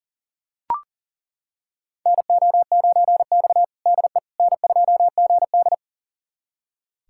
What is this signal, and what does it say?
Morse code, heard as a keyed tone, about 40 wpm, 700 Hz. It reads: NO9X DE N2GD
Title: USA Callsign DE Callsign - 4 Characters - 40wpm